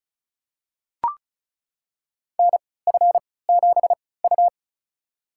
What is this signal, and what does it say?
Morse code, heard as a keyed tone, about 34 wpm, spaced as for 20 wpm, 690 Hz. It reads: NF7U
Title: USA Call Signs - 3x Character Spacing - 35wpm